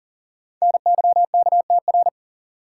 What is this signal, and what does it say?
Morse code, heard as a keyed tone, about 39 wpm, 700 Hz. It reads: NYKTR
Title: Code Groups - 5 Letters - 40wpm